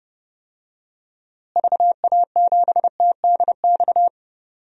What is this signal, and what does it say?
Morse code, heard as a keyed tone, about 32 wpm, 700 Hz. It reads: VA7TDX
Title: DXCC-100 Call Signs - 6 Characters - Repeated 2x - 30wpm